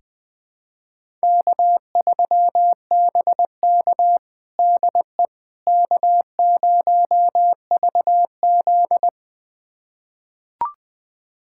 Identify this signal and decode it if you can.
Morse code, heard as a keyed tone, about 21 wpm, 700 Hz. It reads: K3BK DE K0VZ
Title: USA Callsign DE Callsign - 4 Characters - 20wpm